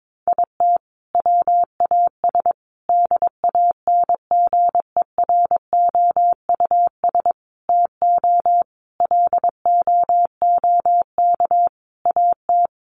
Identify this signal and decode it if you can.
Morse code, heard as a keyed tone, about 22 wpm, 700 Hz. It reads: IT WAS DANGEROUS TO LOOK AT
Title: Sentences from Top 1300 Words 22wpm